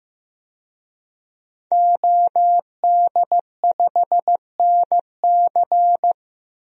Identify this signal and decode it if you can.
Morse code, heard as a keyed tone, about 15 wpm, 700 Hz. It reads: OD5NC
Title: DXCC-100 Call Signs - 5 Characters - Repeated 2x - 15wpm